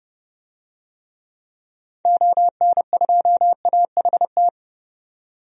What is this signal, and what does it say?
Morse code, heard as a keyed tone, about 30 wpm, 700 Hz. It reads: ON2AHT
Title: DXCC-100 Call Signs - 6 Characters - 30wpm